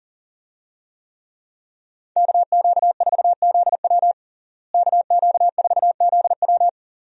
Morse code, heard as a keyed tone, about 40 wpm, 700 Hz. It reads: KQ4ZW KQ4ZW
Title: USA Call Signs - 5 Characters - Repeated 2x - 40wpm